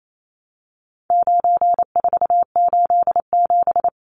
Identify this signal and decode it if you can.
Morse code, heard as a keyed tone, about 28 wpm, 700 Hz. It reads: 9487
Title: Numbers - 4-Digit 28wpm